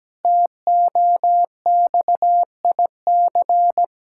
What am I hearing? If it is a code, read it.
Morse code, heard as a keyed tone, about 17 wpm, 700 Hz. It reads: TOXIC
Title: Pangram Challenge v2 - 17wpm